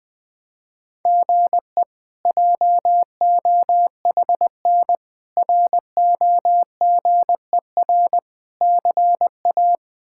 Morse code, heard as a keyed tone, about 20 wpm, 700 Hz. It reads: GE JOHN ROGER CA
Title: K1USN SST QSO Practice - 20wpm